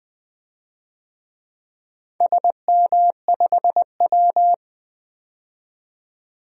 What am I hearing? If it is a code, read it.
Morse code, heard as a keyed tone, about 20 wpm, 700 Hz. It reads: SM5W
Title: DXCC-100 Call Signs - 4 Characters - 20wpm